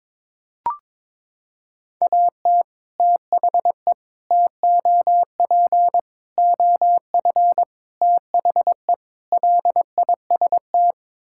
Morse code, heard as a keyed tone, about 22 wpm, 700 Hz. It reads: AT THE TOP OF THE LIST
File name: practice / 22wpm / Sets of 6 Words - 22wpm